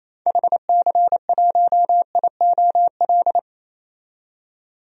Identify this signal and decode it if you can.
Morse code, heard as a keyed tone, about 28 wpm, 700 Hz. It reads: HC1IOL
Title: DXCC-100 Call Signs - 6 Characters - 28wpm